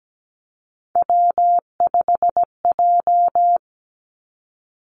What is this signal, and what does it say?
Morse code, heard as a keyed tone, about 17 wpm, 700 Hz. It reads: W5J
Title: Three Challenging Letter-Number Combinations 17wpm